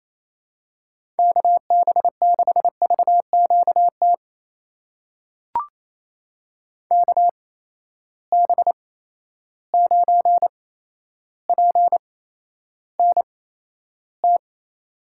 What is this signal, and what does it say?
Morse code, heard as a keyed tone, about 28 wpm, spaced as for 12 wpm, 700 Hz. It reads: KB6VQT KB9PNT
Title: USA Call Signs - 6 Characters - 8x Character Spacing - 28wpm